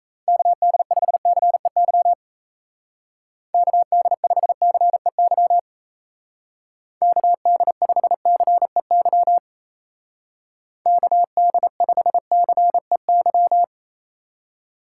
Morse code, heard as a keyed tone, about 34 wpm, 700 Hz. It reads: KD5CEY KD5CEY KD5CEY KD5CEY
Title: Speed Racing - USA 2x3 Call Signs 28wpm